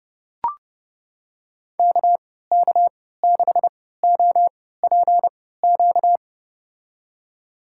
Morse code, heard as a keyed tone, about 30 wpm, spaced as for 17 wpm, 700 Hz. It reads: KK6OPQ
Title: USA Call Signs - 6 Characters - 3x Character Spacing - 30wpm